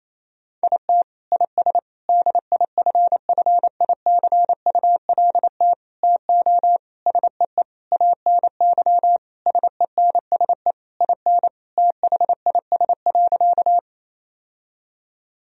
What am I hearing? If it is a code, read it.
Morse code, heard as a keyed tone, about 28 wpm, 700 Hz. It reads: IT IS DIFFICULT TO SEE ANY SENSE IN THIS.
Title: Sentences from Top 1000 Words - Review Part 4 - 28wpm